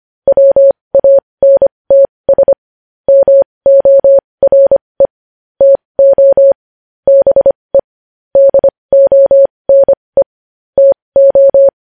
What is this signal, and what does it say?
Morse code, heard as a keyed tone, about 25 wpm, spaced as for 20 wpm, 550 Hz. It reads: WANTS MORE TO BE DONE TO
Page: news-headlines-in-morse-code-at-20-wpm